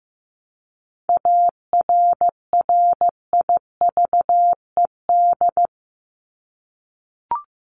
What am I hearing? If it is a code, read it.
Morse code, heard as a keyed tone, about 15 wpm, 700 Hz. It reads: ARRIVED
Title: ED Suffix - Top 100 Words with Base Words 15wpm